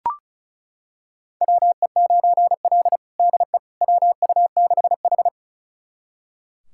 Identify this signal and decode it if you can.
Morse code, heard as a keyed tone, about 36 wpm, 700 Hz. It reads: WE9L DE WU6H
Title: USA Callsign DE Callsign - 4 Characters - 35wpm